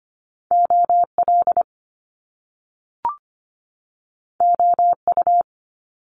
Morse code, heard as a keyed tone, about 25 wpm, 700 Hz. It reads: OL OU
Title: Top Two Letter Combinations in Words 25wpm